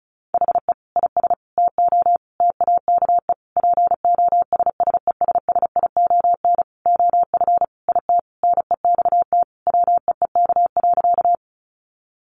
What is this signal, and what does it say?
Morse code, heard as a keyed tone, about 34 wpm, 700 Hz. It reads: HE IS TO TAKE POSSESSION OF IT NEXT WEEK.
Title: Sentences from Top 1400 Words 35wpm